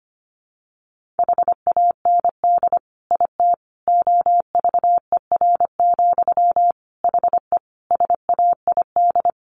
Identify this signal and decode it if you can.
Morse code, heard as a keyed tone, about 25 wpm, 700 Hz. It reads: HAND IT OVER, HE SAID
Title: Sentences from Top 1500 Words 25wpm